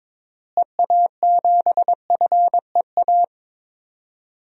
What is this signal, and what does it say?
Morse code, heard as a keyed tone, about 21 wpm, 700 Hz. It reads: EA7FEA
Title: DXCC-100 Call Signs - 6 Characters - Repeated 3x - 22wpm